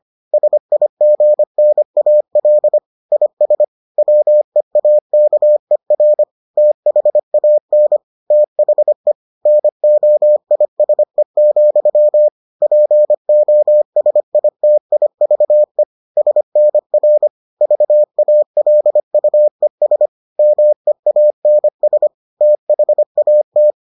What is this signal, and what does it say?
Morse code, heard as a keyed tone, about 25 wpm, 600 Hz. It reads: SIGNAL IS WEAKER THAN THE NOISE, POSITIVE SNR VALUES MEANS THAT